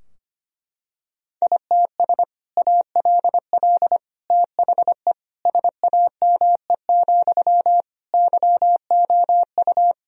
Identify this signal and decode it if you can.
Morse code, heard as a keyed tone, about 25 wpm, 700 Hz. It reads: ITS ALL THE SAME, YOU
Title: Sentences from Top 1000 Words - Review Part 3 - 25wpm